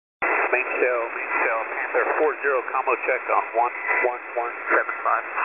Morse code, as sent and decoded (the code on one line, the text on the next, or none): none